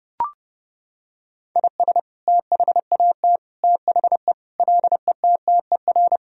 Morse code, heard as a keyed tone, about 30 wpm, 700 Hz. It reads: IS THAT THE LETTER
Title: Sentences from Top 1000 Words - Review Part 2 - 30wpm